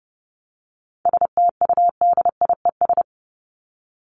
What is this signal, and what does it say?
Morse code, heard as a keyed tone, about 30 wpm, 700 Hz. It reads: STUDIES
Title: ES:S Suffix - Top 100 Words with Base Words 30wpm